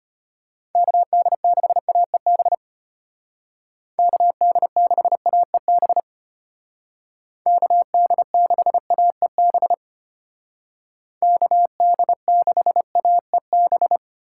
Morse code, heard as a keyed tone, about 33 wpm, 700 Hz. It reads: KD6AEB KD6AEB KD6AEB KD6AEB